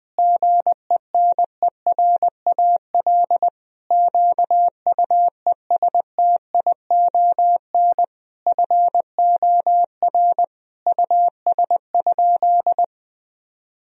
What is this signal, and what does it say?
Morse code, heard as a keyed tone, about 21 wpm, 700 Hz. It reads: GENERAL QUESTION FOR US?
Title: Sentences from Top 1000 Words - Review Part 3 - 20wpm